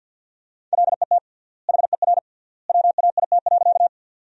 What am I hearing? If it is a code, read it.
Morse code, heard as a keyed tone, about 53 wpm, 700 Hz. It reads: LET HER WAIT.
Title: 3x Word-Spacing - Sentences from Top 500 Words - 50wpm